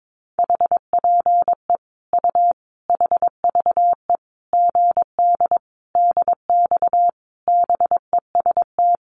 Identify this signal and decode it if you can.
Morse code, heard as a keyed tone, about 22 wpm, 700 Hz. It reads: HPE U HVE GD DX BEST
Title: Common QSO Phrases 22wpm